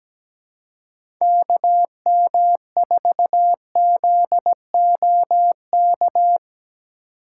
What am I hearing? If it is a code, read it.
Morse code, heard as a keyed tone, about 17 wpm, 700 Hz. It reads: KM4ZOK
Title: USA Call Signs - 6 Characters - 4x Character Spacing - 17wpm